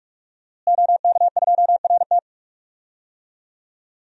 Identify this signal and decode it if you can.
Morse code, heard as a keyed tone, about 46 wpm, 700 Hz. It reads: OK2RT
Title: DXCC-100 Call Signs - 5 Characters - Repeated 3x - 45wpm